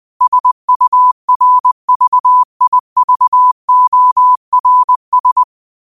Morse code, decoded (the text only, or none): SURVIVORS